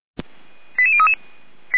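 Five-tone selective call (ZVEI), heard as quick runs of five tones, tones of about 70 ms. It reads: 88033
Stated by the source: ZVEI five-tone selective call received by SDR, from the fire brigade server feuerwehr-orlamuende.org.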